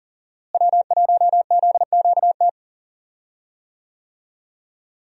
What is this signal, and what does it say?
Morse code, heard as a keyed tone, about 40 wpm, 700 Hz. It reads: W1ZQT
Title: USA Call Signs - 5 Characters - Repeated 2x - 40wpm